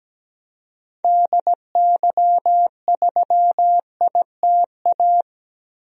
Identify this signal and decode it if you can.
Morse code, heard as a keyed tone, about 17 wpm, 700 Hz. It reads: DY3ITA
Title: DXCC-100 Call Signs - 6 Characters - Repeated 3x - 17wpm